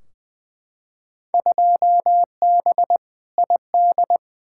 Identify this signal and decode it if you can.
Morse code, heard as a keyed tone, about 20 wpm, 700 Hz. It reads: 2B ID
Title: Field Day Exchanges Encore - 20wpm